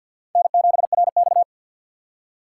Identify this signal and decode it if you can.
Morse code, heard as a keyed tone, about 63 wpm, 700 Hz. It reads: N7RX